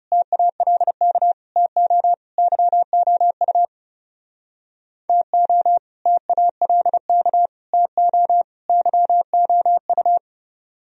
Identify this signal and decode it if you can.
Morse code, heard as a keyed tone, about 36 wpm, 700 Hz. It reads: TALK TO YOU TO TALK TO YOU
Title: Speed-Racing - Sets of 4 Words 30wpm